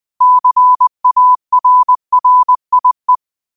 Morse code, heard as a keyed tone, about 20 wpm, 1000 Hz. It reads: CARRIE